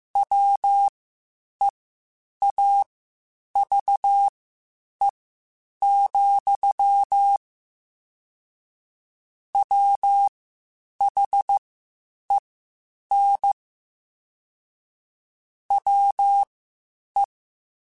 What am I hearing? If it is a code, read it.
Morse code, heard as a keyed tone, about 15 wpm, spaced as for 8 wpm, 800 Hz. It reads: WEAVE, WHEN WE
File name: Quotes Spaced / Quotes Spaced-mp3s / Deceive sp @13